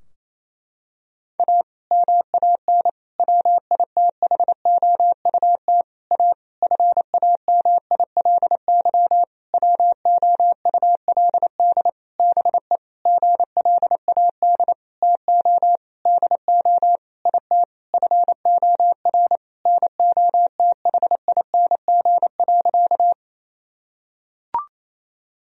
Morse code, heard as a keyed tone, about 27 wpm, 700 Hz. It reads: A MAN WITHOUT A FAMILY WOULD BE GLAD TO DO IT FOR NOTHING.
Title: Sentences from Top 1000 Words - Review Part 3 - 28wpm